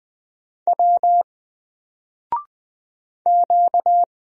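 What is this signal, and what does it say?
Morse code, heard as a keyed tone, about 20 wpm, 700 Hz. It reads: W Q
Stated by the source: Single letter and number Morse drill at 20 wpm, from N4LCD.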